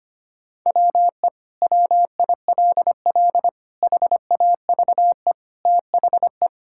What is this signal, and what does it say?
Morse code, heard as a keyed tone, about 25 wpm, 700 Hz. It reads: WE WILL HAVE THE
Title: Sentences from Top 1000 Words - Review Part 6 - 25wpm